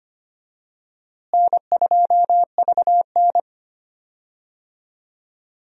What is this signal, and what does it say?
Morse code, heard as a keyed tone, about 25 wpm, 700 Hz. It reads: N2VN
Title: USA Call Signs - 4 Characters - Repeated 2x - 25wpm